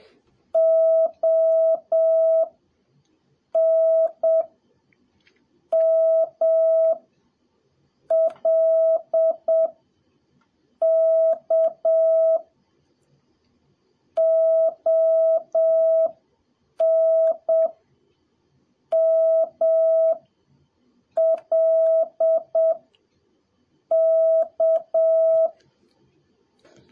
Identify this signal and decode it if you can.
Morse code, heard as a keyed tone, about 7 wpm, 650 Hz. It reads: O N M L K O N M L K
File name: jh8cbh / 2021cw / cw03a